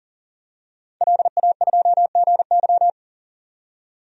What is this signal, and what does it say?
Morse code, heard as a keyed tone, about 40 wpm, 700 Hz. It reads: LA2GY
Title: DXCC-100 Call Signs - 5 Characters - 40wpm